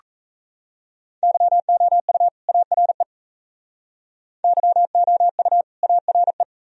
Morse code, heard as a keyed tone, about 37 wpm, 700 Hz. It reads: YOU ARE YOU ARE